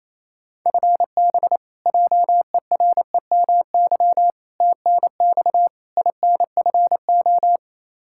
Morse code, heard as a keyed tone, about 28 wpm, 700 Hz. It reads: FB JEREMY TNX INFO
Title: Practice QSO - Exchange 3 - More Info - 28wpm